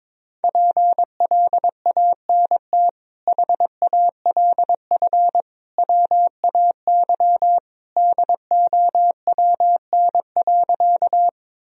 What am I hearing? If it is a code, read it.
Morse code, heard as a keyed tone, about 22 wpm, 700 Hz. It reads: PLANT HALF WAY DOWN.
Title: Sentences from Top 1000 Words - Review Part 1 - 22wpm